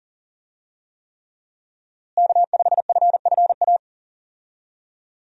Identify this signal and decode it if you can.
Morse code, heard as a keyed tone, about 40 wpm, 700 Hz. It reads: K5FFA
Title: USA Call Signs - 5 Characters - Repeated 3x - 40wpm